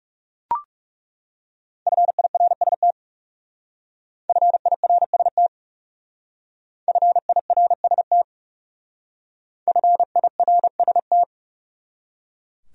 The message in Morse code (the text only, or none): FIRST FIRST FIRST FIRST